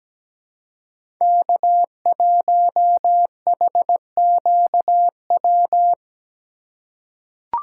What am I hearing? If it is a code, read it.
Morse code, heard as a keyed tone, about 17 wpm, 700 Hz. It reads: K1HQW
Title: USA Call Signs - 5 Characters - 4x Character Spacing - 17wpm